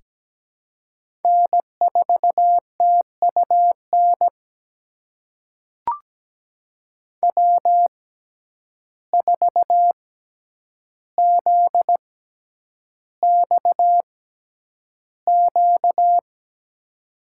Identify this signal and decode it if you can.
Morse code, heard as a keyed tone, about 17 wpm, spaced as for 9 wpm, 700 Hz. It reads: N4TUN W4ZXQ
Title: USA Call Signs - 6x Character Spacing - 17wpm